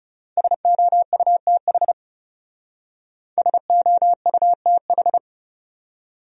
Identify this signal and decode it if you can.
Morse code, heard as a keyed tone, about 36 wpm, 700 Hz. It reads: SOUTH SOUTH
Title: Speed-Racing Top 500 Words 30wpm